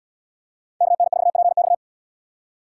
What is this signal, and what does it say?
Morse code, heard as a keyed tone, about 77 wpm, 700 Hz. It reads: KN4PX